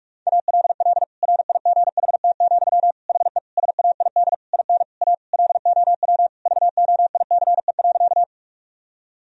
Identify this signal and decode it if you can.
Morse code, heard as a keyed tone, about 46 wpm, 700 Hz. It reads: ALL RIGHT, HE SAID IN A LOW VOICE.
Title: Sentences from Top 1000 Words - Review Part 1 - 45wpm